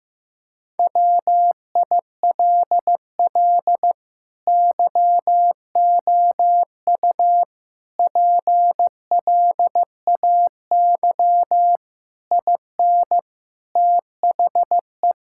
Morse code, heard as a keyed tone, about 15 wpm, 700 Hz. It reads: WILL YOU PLAY IN THE
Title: Sentences from Top 1600 Words 15wpm